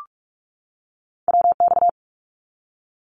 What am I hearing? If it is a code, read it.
Morse code, heard as a keyed tone, about 46 wpm, 700 Hz. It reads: WX
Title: Abbreviations 45wpm